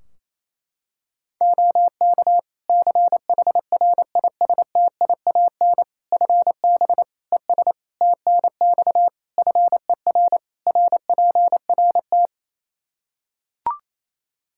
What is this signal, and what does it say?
Morse code, heard as a keyed tone, about 28 wpm, 700 Hz. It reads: OK CHRISTIAN FB ES TNX FER RPRT
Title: Practice QSO - Exchange 2 - Info - 28wpm